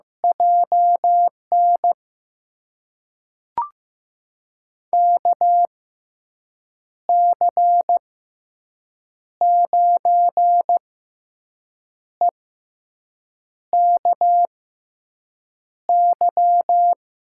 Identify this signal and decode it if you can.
Morse code, heard as a keyed tone, about 15 wpm, spaced as for 6 wpm, 700 Hz. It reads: JN KC9EKY
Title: USA Call Signs - 6 Characters - 6x Character Spacing - 15wpm